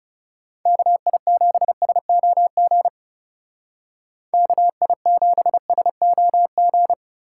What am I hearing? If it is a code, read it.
Morse code, heard as a keyed tone, about 38 wpm, 700 Hz. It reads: KI7SOG KI7SOG